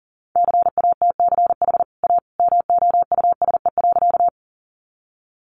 Morse code, heard as a keyed tone, about 40 wpm, 700 Hz. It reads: CATCH A MOUSE.
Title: Sentences from Top 1500 Words 40wpm